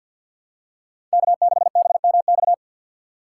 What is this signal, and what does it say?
Morse code, heard as a keyed tone, about 52 wpm, 700 Hz. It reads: K6BMX
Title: USA Call Signs - 5 Characters - 6x Character Spacing - 50wpm